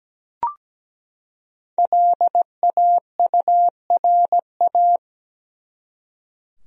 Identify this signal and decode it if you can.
Morse code, heard as a keyed tone, about 17 wpm, 700 Hz. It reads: LAURA
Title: USA Common Names - Max 5 Letters - 17wpm